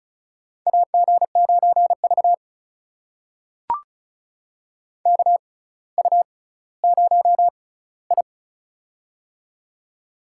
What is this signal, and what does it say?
Morse code, heard as a keyed tone, about 36 wpm, spaced as for 18 wpm, 700 Hz. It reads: AG9V KU0I
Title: USA Call Signs - 4 Characters - 6x Character Spacing - 35wpm